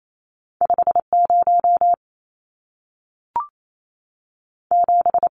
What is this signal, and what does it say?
Morse code, heard as a keyed tone, about 28 wpm, 700 Hz. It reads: 5 0 7
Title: Numbers - 2-Digit 28wpm